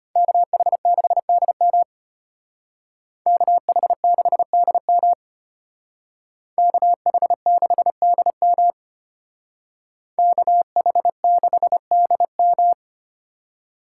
Morse code, heard as a keyed tone, about 31 wpm, 700 Hz. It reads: KH6DM KH6DM KH6DM KH6DM
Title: Speed Racing - USA 2x2 Call Signs 25wpm